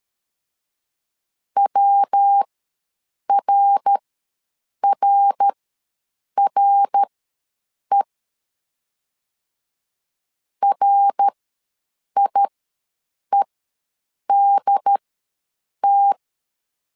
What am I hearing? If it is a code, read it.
Morse code, heard as a keyed tone, about 13 wpm, spaced as for 7 wpm, 800 Hz. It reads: WRRRE RIEDT
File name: LEZIONI / 3 MWRO / 3 MWR ws 12wpm 1809